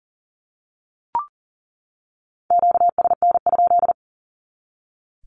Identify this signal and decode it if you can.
Morse code, heard as a keyed tone, about 40 wpm, 700 Hz. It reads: QSN?
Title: Q-Codes 40wpm